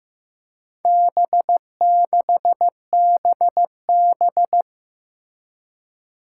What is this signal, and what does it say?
Morse code, heard as a keyed tone, about 15 wpm, 700 Hz. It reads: B6BB
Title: Four Challenging Letter-Number Combinations 15wpm